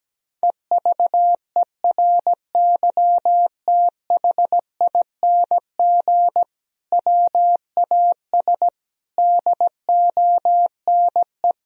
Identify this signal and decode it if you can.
Morse code, heard as a keyed tone, about 17 wpm, 700 Hz. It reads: EVERYTHING WAS DONE
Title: Sentences from Top 1000 Words - Review Part 2 - 17wpm